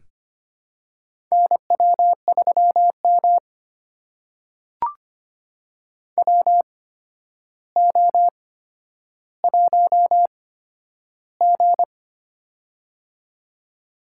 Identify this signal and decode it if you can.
Morse code, heard as a keyed tone, about 25 wpm, spaced as for 11 wpm, 700 Hz. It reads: NW3M WO1G
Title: USA Call Signs - 4 Characters - 8x Character Spacing - 25wpm